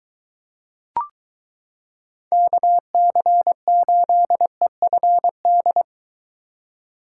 Morse code, heard as a keyed tone, about 22 wpm, 700 Hz. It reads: KC8EFD